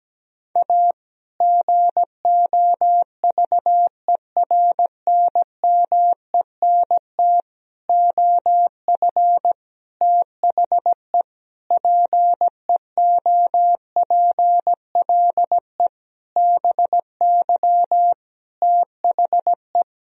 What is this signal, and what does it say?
Morse code, heard as a keyed tone, about 17 wpm, 700 Hz. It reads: A GOVERNMENT OF THE PEOPLE BY THE
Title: Sentences from Top 1000 Words - Review Part 3 - 17wpm